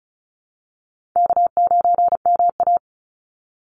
Morse code, heard as a keyed tone, about 34 wpm, 700 Hz. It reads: K9MA
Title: Easy Call Signs Repeated 3x - 35wpm